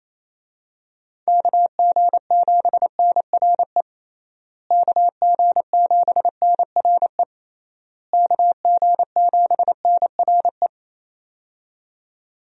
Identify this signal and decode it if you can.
Morse code, heard as a keyed tone, about 28 wpm, 700 Hz. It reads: KG7NRE KG7NRE KG7NRE
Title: USA Call Signs - 6 Characters - Repeated 3x - 28wpm